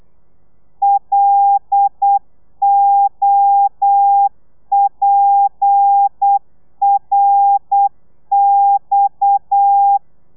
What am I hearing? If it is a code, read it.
Morse code, heard as a keyed tone, about 8 wpm, 800 Hz. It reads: LOPRX